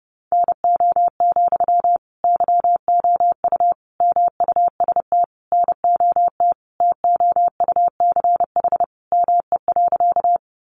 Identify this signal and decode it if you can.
Morse code, heard as a keyed tone, about 28 wpm, 700 Hz. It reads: NO, YOU MUST NOT TOUCH ME.
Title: Sentences from Top 1200 Words 30wpm